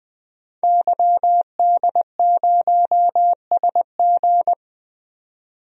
Morse code, heard as a keyed tone, about 20 wpm, 700 Hz. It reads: YD0SG
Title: DXCC-100 Call Signs - 5 Characters - 20wpm